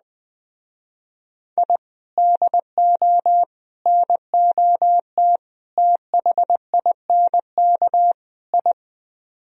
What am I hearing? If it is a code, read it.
Morse code, heard as a keyed tone, about 21 wpm, 700 Hz. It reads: I DO NOT THINK I